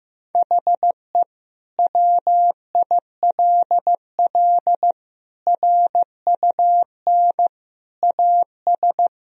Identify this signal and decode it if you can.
Morse code, heard as a keyed tone, about 15 wpm, 700 Hz. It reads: HE WILL RUN AS
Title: Sentences from Top 1000 Words - Review Part 2 - 15wpm